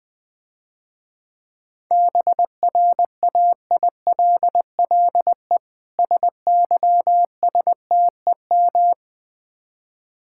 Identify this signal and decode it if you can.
Morse code, heard as a keyed tone, about 21 wpm, 700 Hz. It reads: BRAILLE SYSTEM
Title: Difficult Words - Speed-Racing with Hints - 20wpm